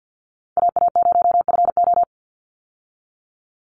Morse code, heard as a keyed tone, about 49 wpm, 700 Hz. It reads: AA0FO
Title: Call Signs Repeated 3x - 50wpm